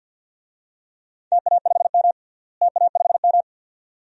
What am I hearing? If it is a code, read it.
Morse code, heard as a keyed tone, about 51 wpm, 700 Hz. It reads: TA5M TA5M
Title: DXCC-100 Call Signs - 4 Characters - Repeated 2x - 50wpm